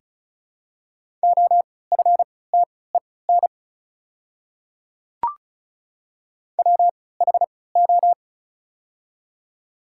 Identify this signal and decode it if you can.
Morse code, heard as a keyed tone, about 34 wpm, spaced as for 20 wpm, 700 Hz. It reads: OFTEN WHO
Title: 3x Character Spacing - Top 500 Words - 35wpm